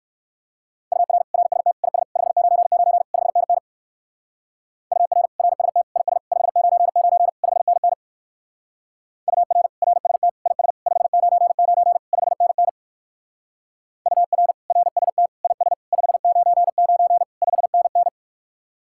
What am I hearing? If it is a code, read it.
Morse code, heard as a keyed tone, about 58 wpm, 700 Hz. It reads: UR RST IS 599 5NN UR RST IS 599 5NN UR RST IS 599 5NN UR RST IS 599 5NN